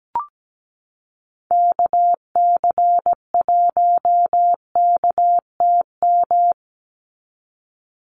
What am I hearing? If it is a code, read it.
Morse code, heard as a keyed tone, about 17 wpm, 700 Hz. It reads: KC1KTM
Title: USA Call Signs 17wpm